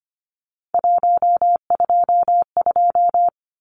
Morse code, heard as a keyed tone, about 25 wpm, 700 Hz. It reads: 122
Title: Numbers - 3-Digit 25wpm